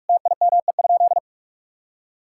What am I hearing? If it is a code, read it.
Morse code, heard as a keyed tone, about 46 wpm, 700 Hz. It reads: TIME?
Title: Sentences from Top 1000 Words - Review Part 6 - 45wpm